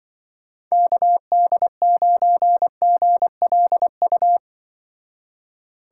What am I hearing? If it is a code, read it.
Morse code, heard as a keyed tone, about 25 wpm, 700 Hz. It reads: KD9GLU